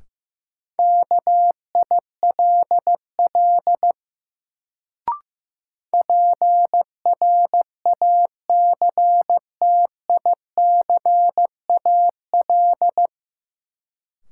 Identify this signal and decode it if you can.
Morse code, heard as a keyed tone, about 15 wpm, 700 Hz. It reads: KILL PRACTICAL
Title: Sentences from Top 1400 Words - New Word Drill Alt - 15wpm